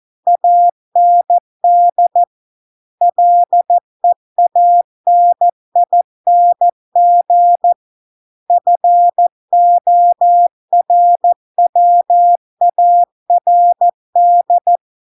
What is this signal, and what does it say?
Morse code, heard as a keyed tone, about 14 wpm, 700 Hz. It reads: AND LEANING FORWARD